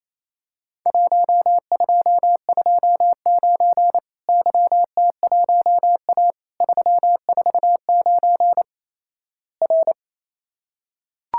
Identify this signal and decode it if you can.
Morse code, heard as a keyed tone, about 28 wpm, 700 Hz. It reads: 1229 YT1A 349 R
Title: Worked All Europe DX Contest - QTC Exchange - 28wpm